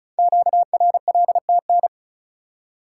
Morse code, heard as a keyed tone, about 36 wpm, 700 Hz. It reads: QRLTN
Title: Code Groups - 5 Letters - 35wpm